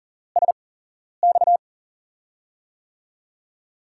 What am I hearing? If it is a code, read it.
Morse code, heard as a keyed tone, about 40 wpm, 700 Hz. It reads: S X
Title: USA Call Signs - 5 Characters - 8x Character Spacing - 40wpm